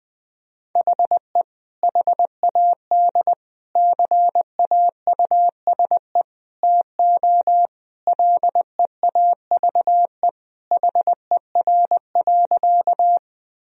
Morse code, heard as a keyed tone, about 20 wpm, 700 Hz. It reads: HE HAD CAUSE TO LEAVE HER.
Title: Sentences from Top 1000 Words - Review Part 1 - 20wpm